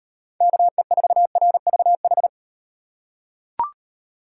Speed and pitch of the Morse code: 36 wpm, 700 Hz

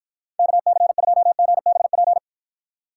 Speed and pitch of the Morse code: 53 wpm, 700 Hz

